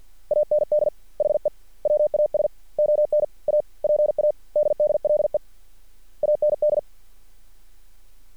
Morse code, heard as a keyed tone, about 50 wpm, 600 Hz. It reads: AND HE WAS ON A PADDLE AND
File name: archived / CRSnet